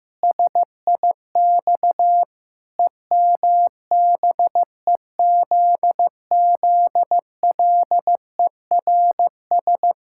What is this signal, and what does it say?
Morse code, heard as a keyed tone, about 15 wpm, 700 Hz. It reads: SIX EMBEZZLERS
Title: Pangram Challenge v2 - 15wpm